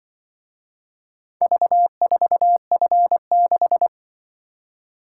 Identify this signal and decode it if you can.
Morse code, heard as a keyed tone, about 24 wpm, 700 Hz. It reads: V4F6
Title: Speed-Racing - Four Letter-Number Combinations - 20wpm